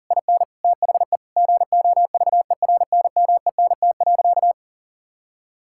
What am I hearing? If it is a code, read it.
Morse code, heard as a keyed tone, about 37 wpm, 700 Hz. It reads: IN THE GOVERNMENT.
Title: Sentences from Top 1000 Words - Review Part 6 - 40wpm